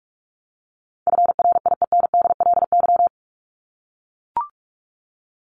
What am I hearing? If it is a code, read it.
Morse code, heard as a keyed tone, about 44 wpm, 700 Hz. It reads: FRIENDLY
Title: Top 1751-2000 Words 45wpm